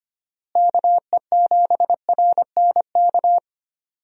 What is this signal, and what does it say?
Morse code, heard as a keyed tone, about 25 wpm, 700 Hz. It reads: KE7RNK
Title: USA Call Signs - 6 Characters - Repeated 2x - 25wpm